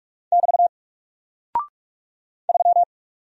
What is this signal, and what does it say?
Morse code, heard as a keyed tone, about 44 wpm, 700 Hz. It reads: <BT> 3
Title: Single Character (Letter:Number:Punctuation:Prosign) Alt 45wpm